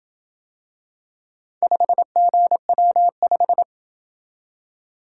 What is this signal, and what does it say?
Morse code, heard as a keyed tone, about 27 wpm, 700 Hz. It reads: 5GW5